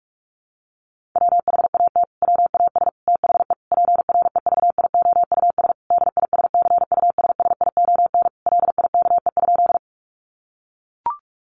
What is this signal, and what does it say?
Morse code, heard as a keyed tone, about 44 wpm, 700 Hz. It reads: WHAT WAS THE PREVIOUS DISCUSSION LIKE?
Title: Sentences from Top 1500 Words 45wpm